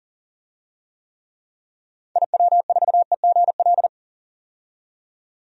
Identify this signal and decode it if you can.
Morse code, heard as a keyed tone, about 40 wpm, 700 Hz. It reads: IW4EGL
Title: DXCC-100 Call Signs - 6 Characters - 40wpm